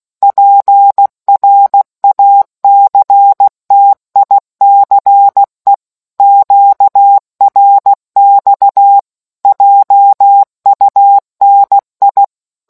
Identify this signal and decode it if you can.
Morse code, heard as a keyed tone, about 16 wpm, 800 Hz. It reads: PRACTICE QRX JUNI